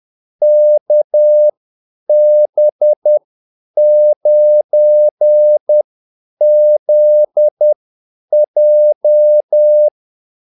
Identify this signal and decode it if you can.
Morse code, heard as a keyed tone, about 10 wpm, spaced as for 8 wpm, 600 Hz. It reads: KB9ZJ
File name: Callsigns-10-8wpm